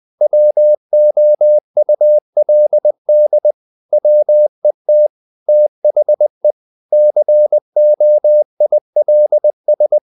Morse code, heard as a keyed tone, about 20 wpm, 600 Hz. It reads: WOULD WET THE COILS
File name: the old man and the sea part 2 20wpm